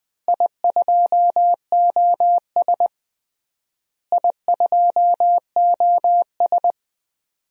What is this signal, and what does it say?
Morse code, heard as a keyed tone, about 20 wpm, 700 Hz. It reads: I2OS I2OS
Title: DXCC-100 Call Signs - 4 Characters - Repeated 3x - 20wpm